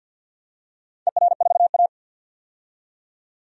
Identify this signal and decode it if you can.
Morse code, heard as a keyed tone, about 51 wpm, 700 Hz. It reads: ER4A
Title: DXCC-100 Call Signs - 4 Characters - 50wpm